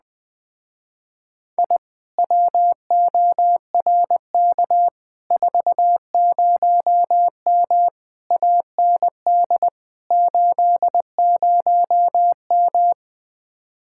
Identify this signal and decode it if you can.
Morse code, heard as a keyed tone, about 20 wpm, 700 Hz. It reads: I WORK 40M AND 80M
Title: Speed-Racing Common QSO Phrases - 17wpm